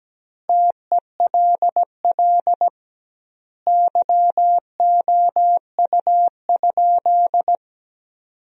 Morse code, heard as a keyed tone, about 17 wpm, 700 Hz. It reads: TELL YOU?
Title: 2x Word-Spacing - Sentences from Top 200 Words - 17wpm